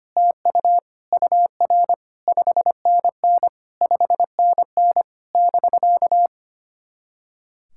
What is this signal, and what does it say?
Morse code, heard as a keyed tone, about 27 wpm, 700 Hz. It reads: TU UR 5NN 5NN <BK>
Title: SOTA QSO Practice 25wpm